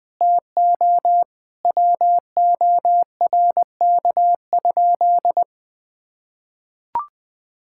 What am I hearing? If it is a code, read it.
Morse code, heard as a keyed tone, about 21 wpm, 700 Hz. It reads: TO WORK?
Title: Sentences from Top 1000 Words - Review Part 6 - 20wpm